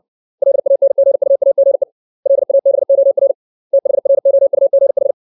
Morse code, heard as a keyed tone, about 60 wpm, 550 Hz. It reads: LANGUAGE LABOR THROUGH
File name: 632 words / 632 Words 60WPM